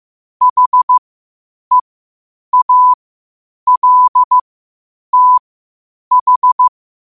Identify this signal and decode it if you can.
Morse code, heard as a keyed tone, about 15 wpm, spaced as for 8 wpm, 1000 Hz. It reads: HEALTH
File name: qotdfiles / 8wpm